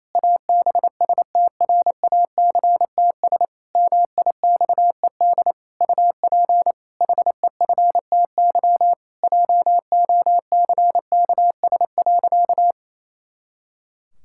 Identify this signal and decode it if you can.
Morse code, heard as a keyed tone, about 28 wpm, 700 Hz. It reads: ABSTRACTS MIXED UP HEFTY JOCKS.
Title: Pangram Challenge v2 - 28wpm